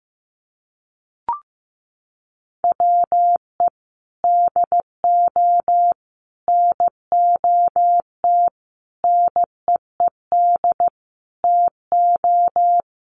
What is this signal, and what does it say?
Morse code, heard as a keyed tone, about 15 wpm, 700 Hz. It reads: WE DO NOT NEED TO
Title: Sets of 5 Words Encore 15wpm